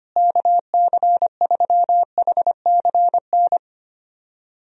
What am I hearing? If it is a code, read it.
Morse code, heard as a keyed tone, about 25 wpm, 700 Hz. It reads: KC3HCN
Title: USA Call Signs - 6 Characters - Repeated 2x - 25wpm